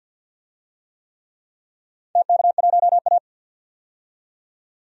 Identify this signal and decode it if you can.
Morse code, heard as a keyed tone, about 51 wpm, 700 Hz. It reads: TK1A
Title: DXCC-100 Call Signs - 4 Characters - 50wpm